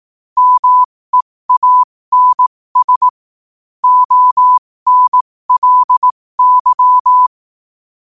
Morse code, heard as a keyed tone, about 18 wpm, spaced as for 15 wpm, 1000 Hz. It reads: MEANS ONLY
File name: qotdfiles / 15wpm